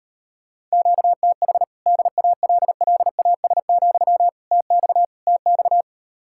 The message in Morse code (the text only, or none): QTH DALLAS, TX TX